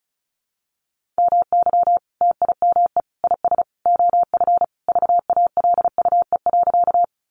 Morse code, heard as a keyed tone, about 34 wpm, 700 Hz. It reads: MY TIME IS OF VALUE.
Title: Sentences from Top 600 Words 35wpm